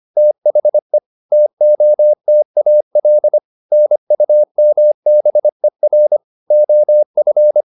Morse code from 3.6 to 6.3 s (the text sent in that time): NUMBER